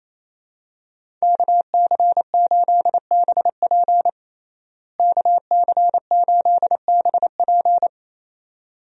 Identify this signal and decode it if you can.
Morse code, heard as a keyed tone, about 28 wpm, 700 Hz. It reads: KC8BP KC8BP
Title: USA Call Signs - 5 Characters - Repeated 2x - 28wpm